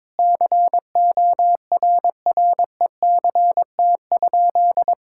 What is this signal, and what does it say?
Morse code, heard as a keyed tone, about 22 wpm, 700 Hz. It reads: CORRECT?
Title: Sentences from Top 1000 Words - Review Part 5 - 22wpm